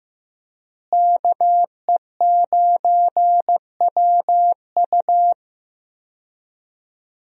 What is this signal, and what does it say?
Morse code, heard as a keyed tone, about 15 wpm, 700 Hz. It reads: KE9WU
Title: USA Call Signs - 5 Characters - Repeated 2x - 15wpm